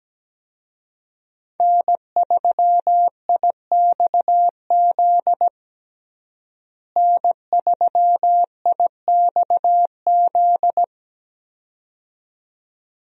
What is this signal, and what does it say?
Morse code, heard as a keyed tone, about 17 wpm, 700 Hz. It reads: N3IXZ N3IXZ
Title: USA Call Signs - 5 Characters - Repeated 2x - 17wpm